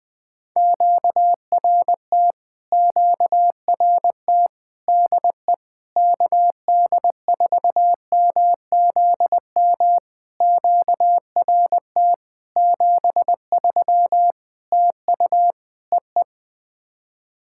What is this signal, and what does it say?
Morse code, heard as a keyed tone, about 20 wpm, 700 Hz. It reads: QRT QRT DE KD4MZM QRT 73 TU EE
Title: POTA QSO Practice - 20wpm